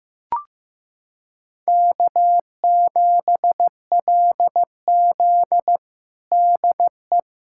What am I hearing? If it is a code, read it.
Morse code, heard as a keyed tone, about 15 wpm, 700 Hz. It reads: K7LZ DE
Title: USA Callsign DE Callsign - 4 Characters - 15wpm